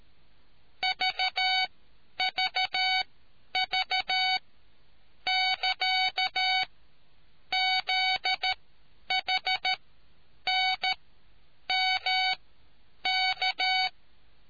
Morse code, heard as a keyed tone, about 13 wpm, spaced as for 8 wpm, 750 Hz. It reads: VVV<KA>ZHNMK